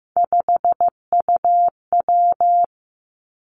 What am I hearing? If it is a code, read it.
Morse code, heard as a keyed tone, about 15 wpm, 700 Hz. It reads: 5UW
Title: Three Challenging Letter-Number Combinations 15wpm